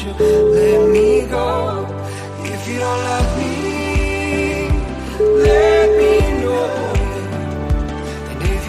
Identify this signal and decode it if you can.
Ringback tone (Europe, 425 Hz), with noise.